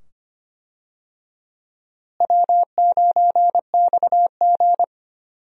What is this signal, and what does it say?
Morse code, heard as a keyed tone, about 25 wpm, 700 Hz. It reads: W9XG
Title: USA Call Signs - 4 Characters - Repeated 2x - 25wpm